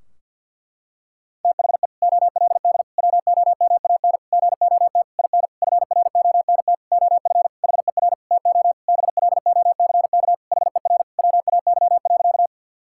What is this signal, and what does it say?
Morse code, heard as a keyed tone, about 50 wpm, 700 Hz. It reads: THE OLD WOMAN GOT IN FRONT OF HER TO BLOCK HER WAY.
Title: Sentences from Top 1600 Words 50wpm